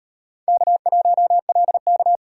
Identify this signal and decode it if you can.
Morse code, heard as a keyed tone, about 38 wpm, 700 Hz. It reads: K1LK